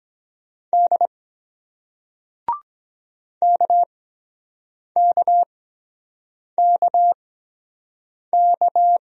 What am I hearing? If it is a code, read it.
Morse code, heard as a keyed tone, about 23 wpm, spaced as for 7 wpm, 700 Hz. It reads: D KKKK